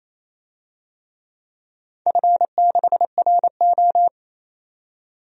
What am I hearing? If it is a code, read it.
Morse code, heard as a keyed tone, about 29 wpm, 700 Hz. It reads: F6RO
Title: DXCC-100 Call Signs - 4 Characters - 28wpm